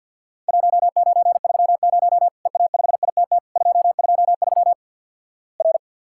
Morse code, heard as a keyed tone, about 50 wpm, 700 Hz. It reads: 1930 EA5ITT 223 R
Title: Worked All Europe DX Contest - QTC Exchange with Repeats - 50wpm